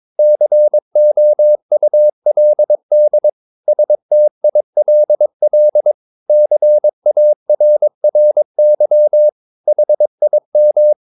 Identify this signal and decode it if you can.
Morse code, heard as a keyed tone, about 22 wpm, 600 Hz. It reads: COULD STILL CARRY HIM
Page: Romanen Never af Ken Follet på engelsk.